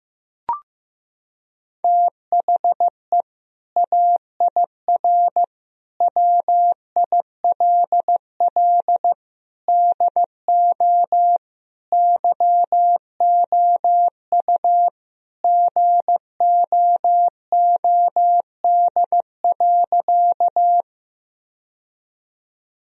Morse code, heard as a keyed tone, about 15 wpm, 700 Hz. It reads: THE AIR WILL DO YOU GOOD.
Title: Sentences from Top 1000 Words - Review Part 1 - 15wpm